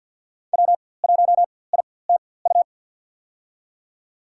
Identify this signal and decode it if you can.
Morse code, heard as a keyed tone, about 47 wpm, spaced as for 23 wpm, 700 Hz. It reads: W1ITU
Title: USA Call Signs - 5 Characters - 4x Character Spacing - 50wpm